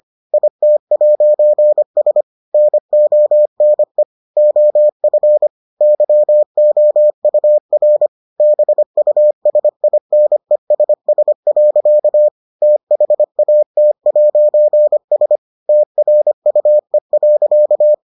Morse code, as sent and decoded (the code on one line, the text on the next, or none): .. - .----. ... / -. --- -. / --- ..-. / -.-- --- ..- .-. / -... ..- ... .. -. ... ... .-.-.- / - .... .- - .----. ... / - .-. ..- . .-.-.-
IT'S NONE OF YOUR BUSINESS. THAT'S TRUE.